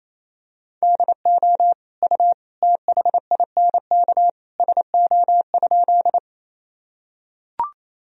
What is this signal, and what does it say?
Morse code, heard as a keyed tone, about 28 wpm, 700 Hz. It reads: DO U THINK SO?
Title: Sentences from Top 200 Words and Abbreviations 28wpm